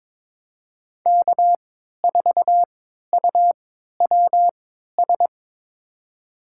Morse code, heard as a keyed tone, about 22 wpm, spaced as for 13 wpm, 700 Hz. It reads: K4UWS
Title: USA Call Signs - 5 Characters - 3x Character Spacing - 22wpm